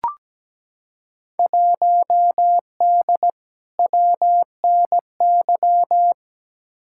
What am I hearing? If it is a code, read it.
Morse code, heard as a keyed tone, about 17 wpm, 700 Hz. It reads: 1D WNY
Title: Field Day Exchanges - 17wpm